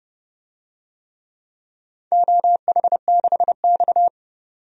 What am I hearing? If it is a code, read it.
Morse code, heard as a keyed tone, about 30 wpm, 700 Hz. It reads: OH6X
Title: DXCC-100 Call Signs - 4 Characters - 30wpm